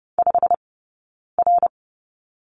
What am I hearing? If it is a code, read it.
Morse code, heard as a keyed tone, about 30 wpm, spaced as for 9 wpm, 690 Hz. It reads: 5R
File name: course / 30:10wpm / 086 Lesson 15d - Number 5 - Two Characters 30wpm